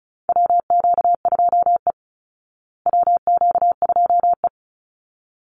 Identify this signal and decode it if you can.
Morse code, heard as a keyed tone, about 34 wpm, 700 Hz. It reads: WQ2E WQ2E
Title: Call Signs Repeated 3x - 35wpm